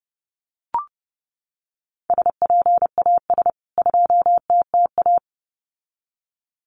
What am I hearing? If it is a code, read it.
Morse code, heard as a keyed tone, about 30 wpm, 700 Hz. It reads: SPAS 2TTA
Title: CWT Exchange 30wpm